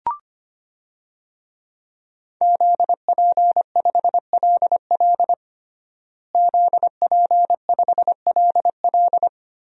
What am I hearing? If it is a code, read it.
Morse code, heard as a keyed tone, about 25 wpm, 700 Hz. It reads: ZP5LL ZP5LL
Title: DXCC-100 Call Signs - 5 Characters - Repeated 3x - 25wpm